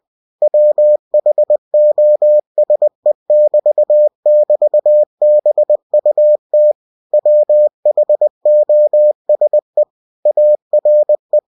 Morse code, heard as a keyed tone, about 20 wpm, 600 Hz. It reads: WHOSE<BT><BT>BUT WHOSE ARE
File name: cw-books / winnie-the-pooh / winnie-20wpm-page-0068